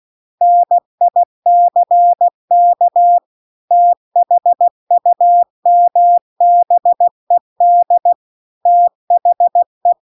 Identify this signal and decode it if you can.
Morse code, heard as a keyed tone, about 16 wpm, 700 Hz. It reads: NICK THUMBED THE